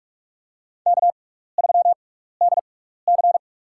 Morse code, heard as a keyed tone, about 43 wpm, spaced as for 15 wpm, 700 Hz. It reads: K3DC
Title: USA Call Signs - 4 Characters - 6x Character Spacing - 45wpm